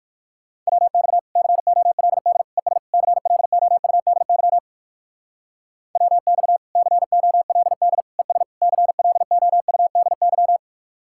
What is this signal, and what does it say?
Morse code, heard as a keyed tone, about 56 wpm, 700 Hz. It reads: WX COLD ES CLOUDY WX COLD ES CLOUDY
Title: Speed-Racing Common QSO Phrases - 45wpm